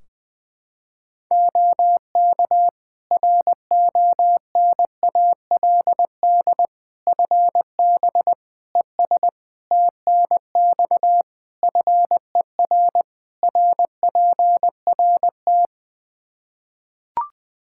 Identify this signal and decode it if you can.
Morse code, heard as a keyed tone, about 20 wpm, 700 Hz. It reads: OK RONALD FB ES TNX FER RPRT
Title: Practice QSO - Exchange 2 - Info - 20wpm